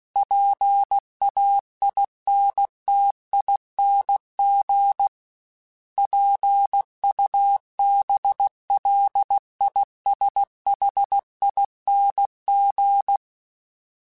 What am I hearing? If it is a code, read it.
Morse code, heard as a keyed tone, about 16 wpm, 800 Hz. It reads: PAINTING PUBLISHING
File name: Special INGs / Special INGs-mp3s / Sngl ing @15